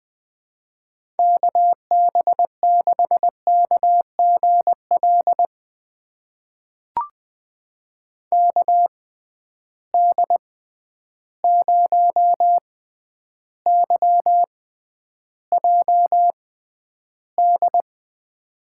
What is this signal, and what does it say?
Morse code, heard as a keyed tone, about 20 wpm, spaced as for 10 wpm, 700 Hz. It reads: KB6KGL KD0YJD
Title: USA Call Signs - 6 Characters - 6x Character Spacing - 20wpm